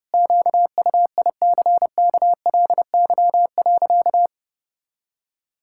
Morse code, heard as a keyed tone, about 32 wpm, 700 Hz. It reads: QUICKLY.
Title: Pangram Challenge v2 - 30wpm